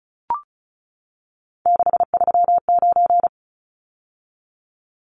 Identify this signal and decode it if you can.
Morse code, heard as a keyed tone, about 34 wpm, 700 Hz. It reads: 639
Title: Numbers - 3-Digit 35wpm